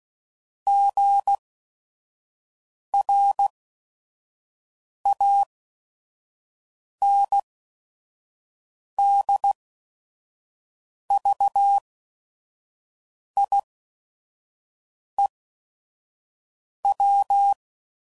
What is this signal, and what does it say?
Morse code, heard as a keyed tone, about 16 wpm, spaced as for 5 wpm, 800 Hz. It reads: GRANDVIEW
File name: CW Code Groups @5WPM / fromSM's / Cities